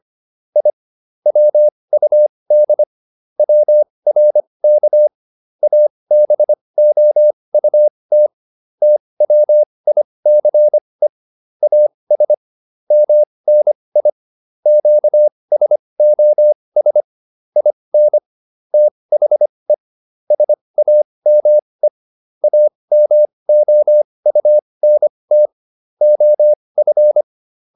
Morse code, heard as a keyed tone, about 25 wpm, spaced as for 20 wpm, 600 Hz. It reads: I WUD WRK ABOUT TWICE AS MNI QSOS IN THE SAME AMOUNT OF